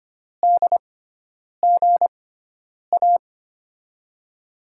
Morse code, heard as a keyed tone, about 25 wpm, spaced as for 9 wpm, 700 Hz. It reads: DGA
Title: USA Call Signs - 6 Characters - 6x Character Spacing - 25wpm